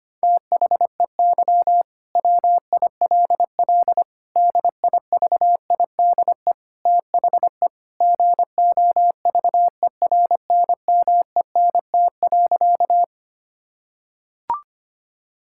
Morse code, heard as a keyed tone, about 25 wpm, 700 Hz. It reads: THEY WILL DIVIDE THE GOVERNMENT.
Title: Sentences from Top 1600 Words 25wpm